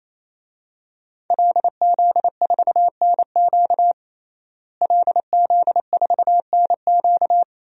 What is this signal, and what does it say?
Morse code, heard as a keyed tone, about 28 wpm, 700 Hz. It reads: LZ4NQ LZ4NQ
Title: DXCC-100 Call Signs - 5 Characters - Repeated 3x - 28wpm